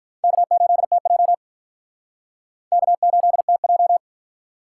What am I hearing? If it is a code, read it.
Morse code, heard as a keyed tone, about 43 wpm, 700 Hz. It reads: K8TJ K8TJ
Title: Speed Racing - USA 1x2 Call Signs 35wpm